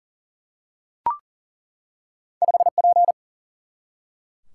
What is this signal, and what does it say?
Morse code, heard as a keyed tone, about 40 wpm, 700 Hz. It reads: 5P
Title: Single-Letter Single-Number 40wpm